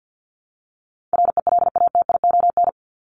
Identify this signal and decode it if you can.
Morse code, heard as a keyed tone, about 50 wpm, 700 Hz. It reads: RELATION
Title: Top 1751-2000 Words 50wpm